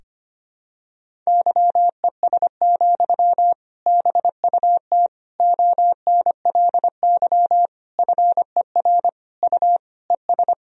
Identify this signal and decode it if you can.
Morse code, heard as a keyed tone, about 25 wpm, 700 Hz. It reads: YES, BUT ONLY FER U ES
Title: Sentences from Top 500 Words and Abbreviations 25wpm